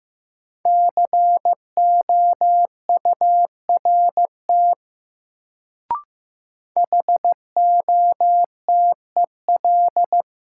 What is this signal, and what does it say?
Morse code, heard as a keyed tone, about 15 wpm, 700 Hz. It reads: COURT HOTEL
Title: Sentences from Top 1600 Words - New Word Drill Alt - 15wpm